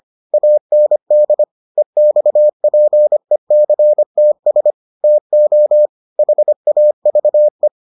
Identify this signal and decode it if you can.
Morse code, heard as a keyed tone, about 25 wpm, 600 Hz. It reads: AND EXPECTS TO HAVE